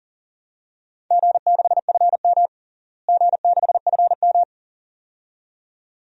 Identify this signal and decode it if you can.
Morse code, heard as a keyed tone, about 40 wpm, 700 Hz. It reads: G6FM G6FM
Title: DXCC-100 Call Signs - 4 Characters - Repeated 2x - 40wpm